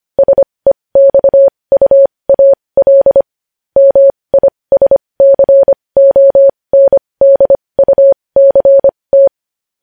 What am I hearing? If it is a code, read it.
Morse code, heard as a keyed tone, about 25 wpm, spaced as for 20 wpm, 550 Hz. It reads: SEXUAL MISCONDUCT
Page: news-headlines-in-morse-code-at-20-wpm